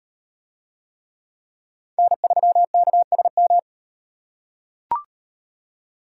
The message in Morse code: N3KSM

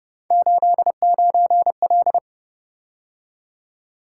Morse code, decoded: 89L

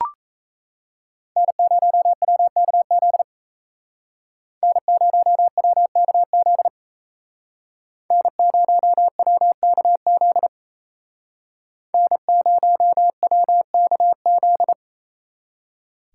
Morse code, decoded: N0WKZ N0WKZ N0WKZ N0WKZ

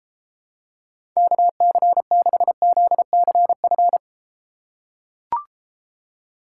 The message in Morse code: KC6ZCF

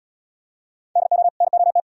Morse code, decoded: DO NOT

